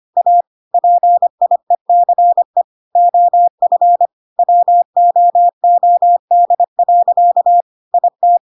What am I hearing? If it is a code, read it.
Morse code, heard as a keyed tone, about 25 wpm, 700 Hz. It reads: A PIECE OF WOOD. IT
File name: Pinocchio / Pinocchio ch1 25 25 ik1hge